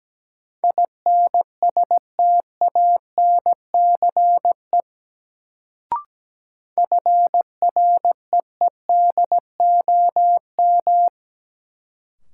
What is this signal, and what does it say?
Morse code, heard as a keyed tone, about 17 wpm, 700 Hz. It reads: INSTANCE FREEDOM
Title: Sentences from Top 800 Words - New Word Drill - 17wpm